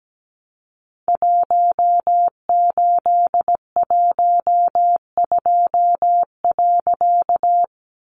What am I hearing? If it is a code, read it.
Morse code, heard as a keyed tone, about 17 wpm, 700 Hz. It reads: 1812.
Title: A Sound Of Thunder 17wpm